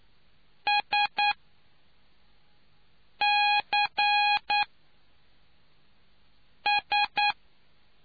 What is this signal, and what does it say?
Morse code, heard as a keyed tone, about 9 wpm, spaced as for 4 wpm, 830 Hz. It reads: SCS